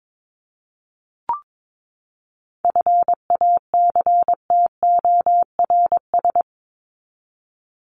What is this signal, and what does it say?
Morse code, heard as a keyed tone, about 22 wpm, 700 Hz. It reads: FACTORS
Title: ES:S Suffix - Top 100 Words with Base Words 22wpm